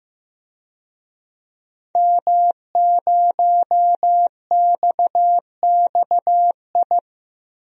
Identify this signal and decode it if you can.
Morse code, heard as a keyed tone, about 15 wpm, 700 Hz. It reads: M0XXI
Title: DXCC-100 Call Signs - 5 Characters - Repeated 3x - 15wpm